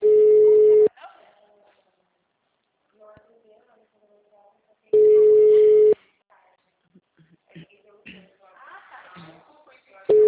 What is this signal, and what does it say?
Ringback tone (Europe, 425 Hz), with speech.